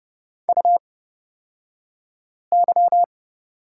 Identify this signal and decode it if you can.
Morse code, heard as a keyed tone, about 30 wpm, spaced as for 5 wpm, 690 Hz. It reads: UY
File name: Lesson 36d - BT - Two Characters 30wpm